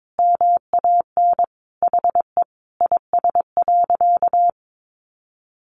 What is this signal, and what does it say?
Morse code, heard as a keyed tone, about 22 wpm, 700 Hz. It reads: MAN HE IS.
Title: Sentences from Top 1100 Words 22wpm